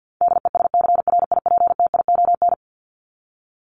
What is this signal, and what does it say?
Morse code, heard as a keyed tone, about 50 wpm, 700 Hz. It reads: DESCRIPTION
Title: Top 1001-1250 Words 50wpm